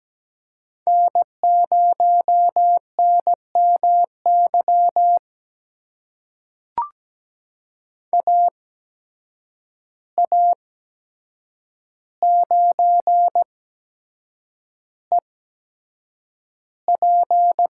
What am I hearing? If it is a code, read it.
Morse code, heard as a keyed tone, about 17 wpm, spaced as for 7 wpm, 700 Hz. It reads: N0NMY AA9EP
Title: USA Call Signs - 5 Characters - 8x Character Spacing - 17wpm